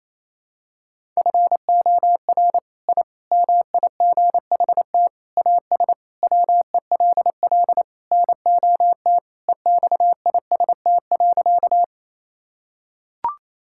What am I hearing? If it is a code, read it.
Morse code, heard as a keyed tone, about 28 wpm, 700 Hz. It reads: FOR I MIGHT AS WELL NOT EXIST.
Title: Sentences from Top 1000 Words - Review Part 5 - 28wpm